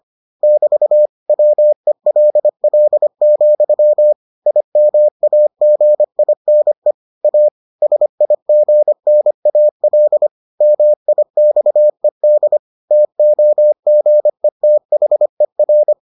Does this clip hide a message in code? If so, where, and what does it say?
Yes: Morse code from 0.4 to 15.9 s, heard as a keyed tone, about 25 wpm, 600 Hz. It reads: <BT> WELL, IMAGINE A SIGNAL MIXED TOGETHER